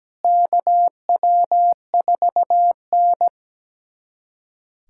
Morse code, heard as a keyed tone, about 17 wpm, 700 Hz. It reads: KW4N
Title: USA Callsign DE Callsign - 4 Characters - 17wpm